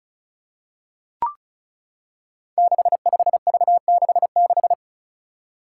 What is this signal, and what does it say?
Morse code, heard as a keyed tone, about 36 wpm, 700 Hz. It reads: 65V66
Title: Five Challenging Letter-Number Combinations 35wpm